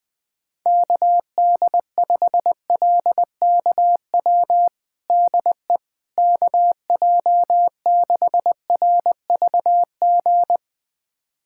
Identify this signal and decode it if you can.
Morse code, heard as a keyed tone, about 20 wpm, 700 Hz. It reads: KD5LKW DE KJ6RVG
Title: USA Callsign DE Callsign - 6 Characters - 20wpm